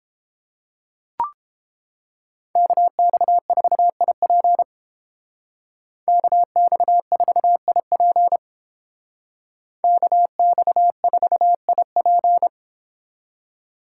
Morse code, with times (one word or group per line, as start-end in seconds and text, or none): 2.6-4.6 KX4IP
6.1-8.4 KX4IP
9.8-12.5 KX4IP